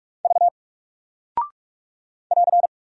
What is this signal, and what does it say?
Morse code, heard as a keyed tone, about 47 wpm, 700 Hz. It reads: V <AR>
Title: Single Character (Letter:Number:Punctuation:Prosign) 45wpm